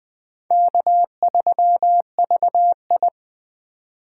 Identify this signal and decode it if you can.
Morse code, heard as a keyed tone, about 21 wpm, 700 Hz. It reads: K3VI
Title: USA Call Signs - 8x Character Spacing - 20wpm